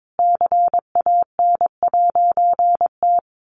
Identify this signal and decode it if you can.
Morse code, heard as a keyed tone, about 22 wpm, 700 Hz. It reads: CAN'T
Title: A Sound Of Thunder 22wpm